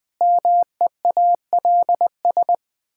Morse code, heard as a keyed tone, about 20 wpm, 700 Hz. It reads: MEALS
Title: Pangram Challenge v2 - 20wpm